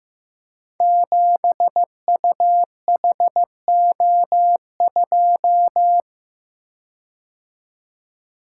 Morse code, heard as a keyed tone, about 15 wpm, 700 Hz. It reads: 7UHO2
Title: Code Groups - 5 Letter:Numbers - 15wpm